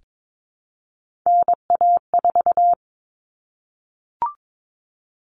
Morse code, heard as a keyed tone, about 22 wpm, 700 Hz. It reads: NA4
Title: All Cut Numbers - 3-Digit 22wpm